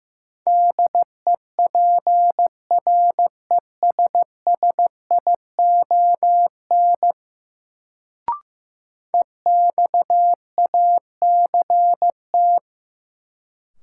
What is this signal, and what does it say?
Morse code, heard as a keyed tone, about 15 wpm, 700 Hz. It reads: DEPRESSION EXACT
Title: Sentences from Top 1300 Words - New Word Drill - 15wpm